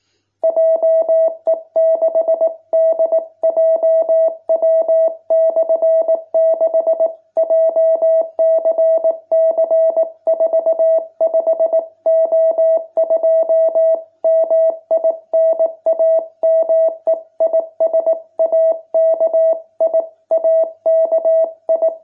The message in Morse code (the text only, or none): JE6DJW/6 JCC45O2 MI NAME IS AKI AKI